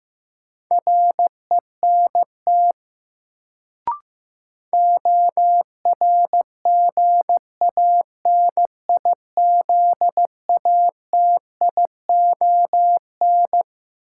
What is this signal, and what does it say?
Morse code, heard as a keyed tone, about 16 wpm, 700 Hz. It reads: RENT ORGANIZATION
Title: Sentences from Top 1000 Words - Review Part 5 - New Word Drill - 15wpm